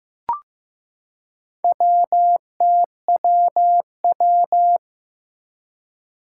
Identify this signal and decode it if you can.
Morse code, heard as a keyed tone, about 15 wpm, 700 Hz. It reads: WTWW
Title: Four Challenging Letter-Number Combinations 15wpm